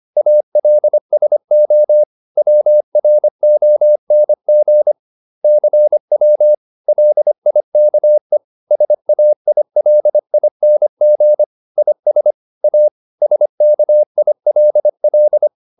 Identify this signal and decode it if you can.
Morse code, heard as a keyed tone, about 25 wpm, 600 Hz. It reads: ALSO WRONG CW LIKE SAILING IS A SKILL